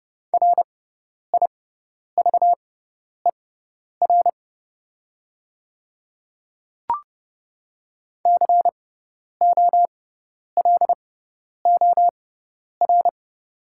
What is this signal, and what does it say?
Morse code, heard as a keyed tone, about 30 wpm, spaced as for 10 wpm, 700 Hz. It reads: RIVER COLOR
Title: 6x Character Spacing - Top 500 Words - 30wpm